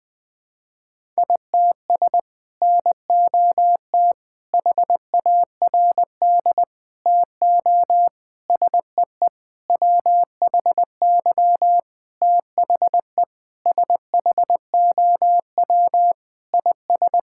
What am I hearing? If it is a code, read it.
Morse code, heard as a keyed tone, about 21 wpm, 700 Hz. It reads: ITS NOT HARD TO SEE WHY THE SHOW IS
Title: Sentences with 6-Word Phrases - 20wpm